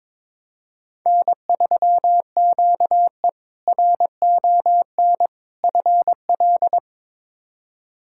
Morse code, heard as a keyed tone, about 21 wpm, 700 Hz. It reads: N3QE RON FL
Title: K1USN SST QSO Practice - 22wpm